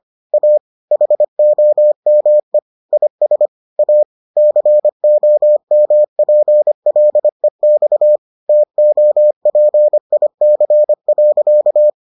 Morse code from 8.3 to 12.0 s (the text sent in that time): TOPIC.